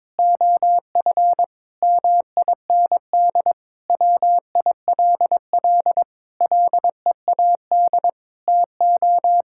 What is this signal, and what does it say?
Morse code, heard as a keyed tone, about 22 wpm, 700 Hz. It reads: OF MIND WILL LEAD TO
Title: Sentences from Top 1000 Words - Review Part 6 - 22wpm